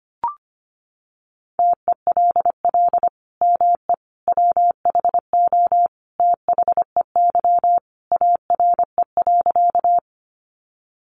Morse code, heard as a keyed tone, about 25 wpm, 700 Hz. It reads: TELL ME WHO THEY ARE.
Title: Sentences from Top 700 Words 25wpm